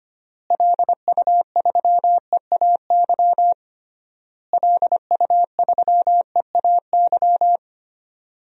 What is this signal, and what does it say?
Morse code, heard as a keyed tone, about 25 wpm, 700 Hz. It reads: LU3EAY LU3EAY
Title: DXCC-100 Call Signs - 6 Characters - Repeated 3x - 25wpm